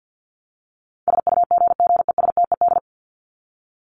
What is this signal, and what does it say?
Morse code, heard as a keyed tone, about 50 wpm, 700 Hz. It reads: SUGGESTED
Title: ED Suffix - Top 100 Words with Base Words 50wpm